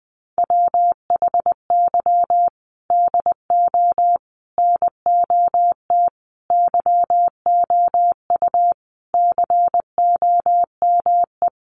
Morse code, heard as a keyed tone, about 20 wpm, 700 Hz. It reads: WHY DO NOT YOU COME
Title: Sets of 5 Words Encore2 20wpm